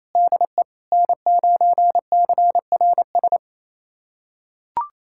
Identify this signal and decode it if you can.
Morse code, heard as a keyed tone, about 28 wpm, 700 Hz. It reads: DE N9CRS
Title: USA Callsign DE Callsign - 5 Characters - 28wpm